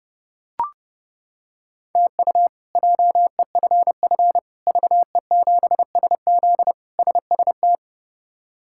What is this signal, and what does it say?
Morse code, heard as a keyed tone, about 32 wpm, 700 Hz. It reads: TU JEFF VE7SZ SST
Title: K1USN SST QSO Practice - 30wpm